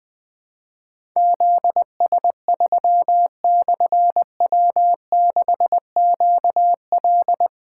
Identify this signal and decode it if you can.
Morse code, heard as a keyed tone, about 20 wpm, 700 Hz. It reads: ZS3/W6QL
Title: Difficult Call Signs 20wpm